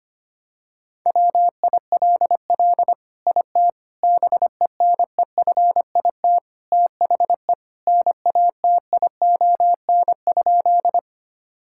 Morse code, heard as a keyed tone, about 25 wpm, 700 Hz. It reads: WILL IT BENEFIT THE NATION?
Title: Sentences from Top 1000 Words - Review Part 6 - 25wpm